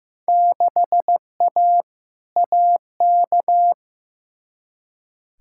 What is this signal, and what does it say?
Morse code, heard as a keyed tone, about 15 wpm, 700 Hz. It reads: 6A AK
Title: Field Day Exchanges Encore - 15wpm